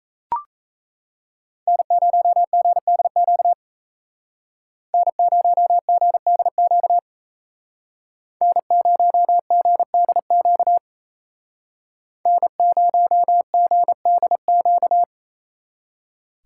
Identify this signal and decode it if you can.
Morse code, heard as a keyed tone, about 36 wpm, 700 Hz. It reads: N0GDQ N0GDQ N0GDQ N0GDQ